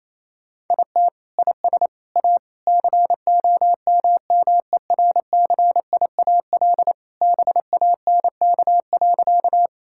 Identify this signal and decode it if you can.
Morse code, heard as a keyed tone, about 27 wpm, 700 Hz. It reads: IT IS A COMMERCIAL BANK.
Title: Sentences from Top 1000 Words - Review Part 6 - 28wpm